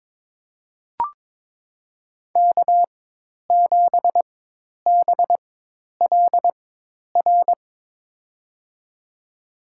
Morse code, heard as a keyed tone, about 22 wpm, spaced as for 10 wpm, 700 Hz. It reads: K7BLR
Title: USA Call Signs - 5 Characters - 4x Character Spacing - 22wpm